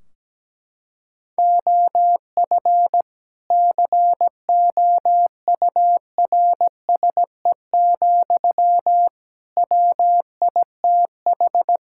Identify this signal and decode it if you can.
Morse code, heard as a keyed tone, about 17 wpm, 700 Hz. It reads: OF COURSE, WITH
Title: Sentences from Top 1600 Words 17wpm